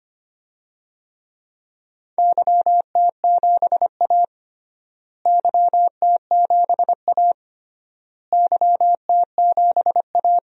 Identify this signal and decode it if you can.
Morse code, heard as a keyed tone, about 25 wpm, 700 Hz. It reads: YT7A YT7A YT7A
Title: DXCC-100 Call Signs - 4 Characters - Repeated 3x - 25wpm